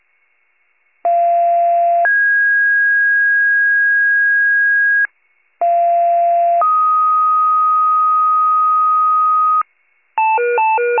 Two-tone page, heard as a one-second tone, then a three-second tone, about 690 then 1650 Hz, about 690 then 1250 Hz, with noise. It is heard from 1.0 to 9.6 s.
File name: pager / 41 FIRE / wcalert